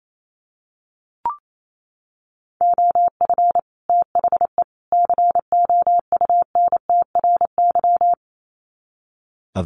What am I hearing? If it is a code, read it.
Morse code, heard as a keyed tone, about 28 wpm, 700 Hz. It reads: OF THE COUNTRY
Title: Sets of 3 Words 28wpm